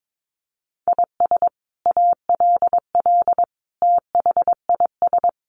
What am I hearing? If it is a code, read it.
Morse code, heard as a keyed tone, about 22 wpm, 700 Hz. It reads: IS ALL THIS
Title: Sentences from Top 1400 Words 22wpm